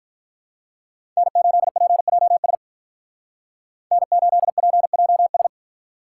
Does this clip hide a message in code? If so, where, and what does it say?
Yes: Morse code from 1.2 to 5.5 s, heard as a keyed tone, about 46 wpm, 700 Hz. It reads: N8PJS N8PJS